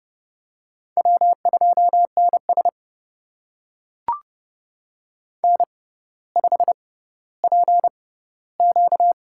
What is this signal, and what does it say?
Morse code, heard as a keyed tone, about 30 wpm, spaced as for 15 wpm, 700 Hz. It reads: W2NS N5PQ
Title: USA Call Signs - 4 Characters - 6x Character Spacing - 30wpm